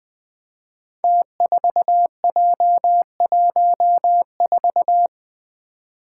Morse code, heard as a keyed tone, about 20 wpm, 700 Hz. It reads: T4J14
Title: Five Challenging Letter-Number Combinations 20wpm